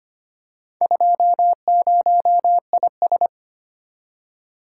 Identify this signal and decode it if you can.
Morse code, heard as a keyed tone, about 25 wpm, 700 Hz. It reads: 20IS
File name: practice / 25wpm / Four Letter-Number Combinations - 25wpm